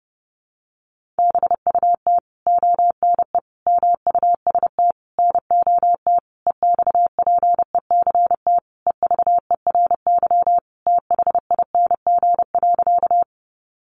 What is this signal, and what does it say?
Morse code, heard as a keyed tone, about 30 wpm, 700 Hz. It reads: BUT ONE MUST NOT EXPECT EVERY THING.
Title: Sentences from Top 1100 Words 30wpm